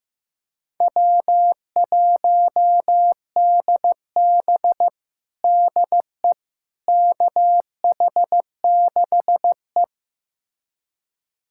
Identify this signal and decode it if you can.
Morse code, heard as a keyed tone, about 15 wpm, 700 Hz. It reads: W1DB DE KH6E
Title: USA Callsign DE Callsign - 4 Characters - 15wpm